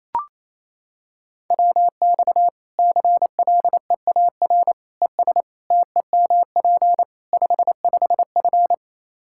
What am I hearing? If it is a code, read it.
Morse code, heard as a keyed tone, about 28 wpm, 700 Hz. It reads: WX CLEAR ES TEMP 55F
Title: Practice QSO - Exchange 2 - Info - 28wpm